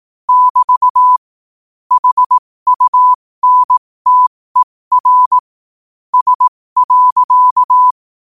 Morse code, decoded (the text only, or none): <BT> HUNTER S.